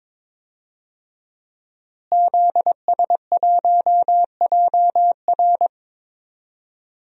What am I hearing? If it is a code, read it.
Morse code, heard as a keyed tone, about 22 wpm, 700 Hz. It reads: ZS1JR
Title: DXCC-100 Call Signs - 5 Characters - Repeated 3x - 22wpm